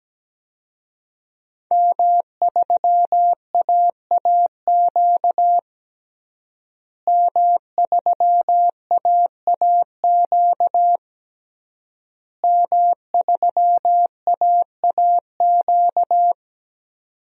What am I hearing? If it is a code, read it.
Morse code, heard as a keyed tone, about 17 wpm, 700 Hz. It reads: M3AAQ M3AAQ M3AAQ
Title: DXCC-100 Call Signs - 5 Characters - Repeated 3x - 17wpm